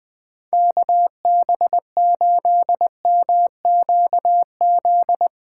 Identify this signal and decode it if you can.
Morse code, heard as a keyed tone, about 21 wpm, 700 Hz. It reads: KB8MQZ